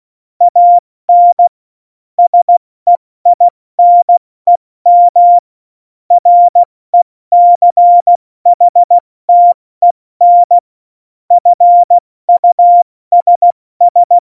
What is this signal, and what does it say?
Morse code, heard as a keyed tone, about 16 wpm, 700 Hz. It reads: AN SEINEM RECHTEN FUSS
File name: Morsekurs / Witze / Witz 20. November 2018, 70 F80